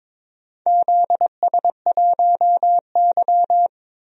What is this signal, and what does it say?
Morse code, heard as a keyed tone, about 22 wpm, 700 Hz. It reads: ZS1Y
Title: DXCC-100 Call Signs - 4 Characters - Repeated 2x - 22wpm